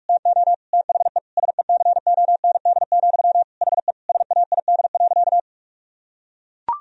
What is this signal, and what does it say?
Morse code, heard as a keyed tone, about 41 wpm, 700 Hz. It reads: TO THE SECOND, HE SAID.
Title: Sentences from Top 1000 Words - Review Part 1 - 45wpm